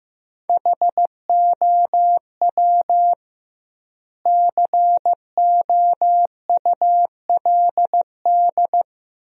Morse code, heard as a keyed tone, about 15 wpm, 700 Hz. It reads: HOW COULD
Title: 2x Word-Spacing - Sentences from Top 400 Words - 15wpm